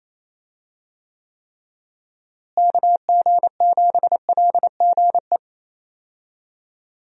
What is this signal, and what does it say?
Morse code, heard as a keyed tone, about 28 wpm, 700 Hz. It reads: KG7LGE
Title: USA Call Signs - 6 Characters - Repeated 3x - 28wpm